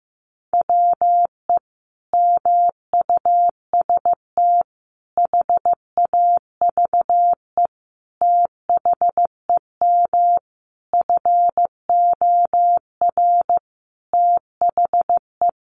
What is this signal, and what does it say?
Morse code, heard as a keyed tone, about 15 wpm, 700 Hz. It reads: WE MUST HAVE THEM FOR THE
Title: Sentences from Top 1200 Words 15wpm